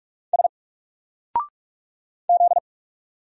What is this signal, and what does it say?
Morse code, heard as a keyed tone, about 46 wpm, 700 Hz. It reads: S Z
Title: Single Character (Letter:Number:Punctuation:Prosign) 45wpm